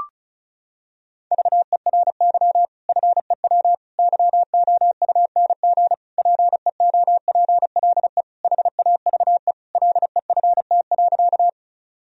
Morse code, heard as a keyed tone, about 34 wpm, 700 Hz. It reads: VERY FEW YOUNG PEOPLE HAVE LEFT.
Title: Sentences from Top 1000 Words - Review Part 1 - 35wpm